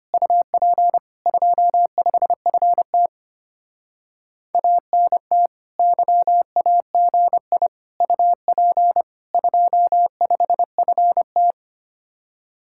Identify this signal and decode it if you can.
Morse code, heard as a keyed tone, about 32 wpm, 700 Hz. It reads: UP 25FT ANT YAGI UP 25FT